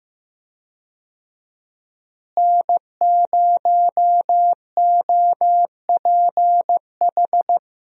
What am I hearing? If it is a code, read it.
Morse code, heard as a keyed tone, about 15 wpm, 700 Hz. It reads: N0OPH
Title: USA Call Signs - 5 Characters - Repeated 3x - 15wpm